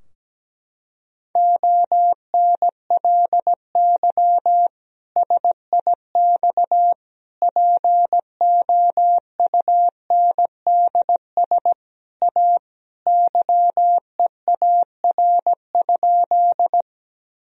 Morse code, heard as a keyed tone, about 17 wpm, 700 Hz. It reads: ONLY SIX POUNDS A YEAR?
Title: Sentences from Top 1000 Words - Review Part 2 - 17wpm